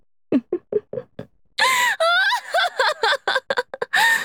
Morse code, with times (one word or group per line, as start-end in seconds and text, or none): none